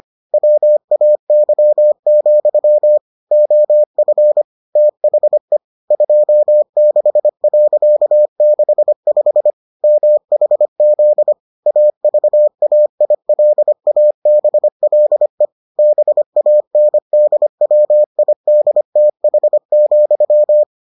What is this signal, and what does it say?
Morse code, heard as a keyed tone, about 25 wpm, 600 Hz. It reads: WAY, OF THE 26.65 MHZ AVAILABLE BANDWIDTH,